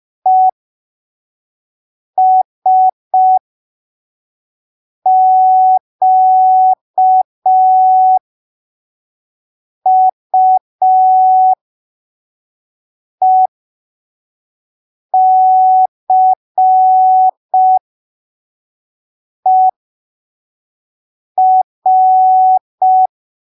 Morse code, heard as a keyed tone, about 5 wpm, spaced as for 3 wpm, 750 Hz. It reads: ESQUECER
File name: arquivos / texto / texto 21 05ppm